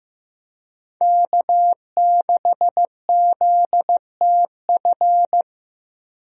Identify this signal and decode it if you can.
Morse code, heard as a keyed tone, about 15 wpm, 700 Hz. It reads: K6ZTF
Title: USA Call Signs - 5 Characters - 15wpm